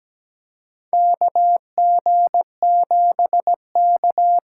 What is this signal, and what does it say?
Morse code, heard as a keyed tone, about 17 wpm, 700 Hz. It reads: KG7K
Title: USA Call Signs - 6x Character Spacing - 17wpm